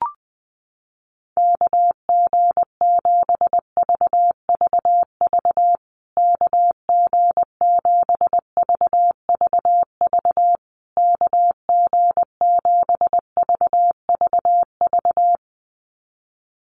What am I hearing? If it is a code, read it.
Morse code, heard as a keyed tone, about 20 wpm, 700 Hz. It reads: KG7VVV KG7VVV KG7VVV
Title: USA Call Signs Repeated 3x 20wpm